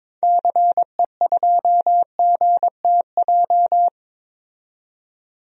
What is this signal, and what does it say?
Morse code, heard as a keyed tone, about 22 wpm, 700 Hz. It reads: CE2GTJ
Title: DXCC-100 Call Signs - 6 Characters - 22wpm